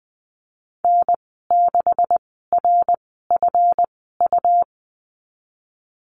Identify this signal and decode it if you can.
Morse code, heard as a keyed tone, about 20 wpm, spaced as for 15 wpm, 700 Hz. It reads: N6RFU
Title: 154 Lesson 25g - Letter p - Call Signs 20wpm